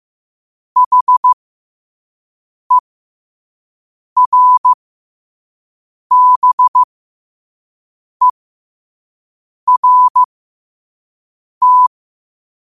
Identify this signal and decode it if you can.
Morse code, heard as a keyed tone, about 15 wpm, spaced as for 5 wpm, 1000 Hz. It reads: HERBERT
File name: qotdfiles / 5wpm